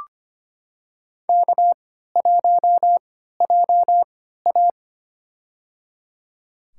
Morse code, heard as a keyed tone, about 25 wpm, spaced as for 14 wpm, 700 Hz. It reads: K1JA
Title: USA Call Signs - 4 Characters - 3x Character Spacing - 25wpm